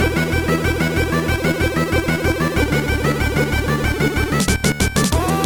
There is no code.